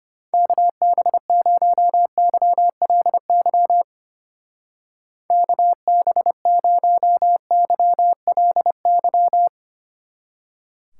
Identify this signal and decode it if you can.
Morse code, heard as a keyed tone, about 32 wpm, 700 Hz. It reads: KB0YLY KB0YLY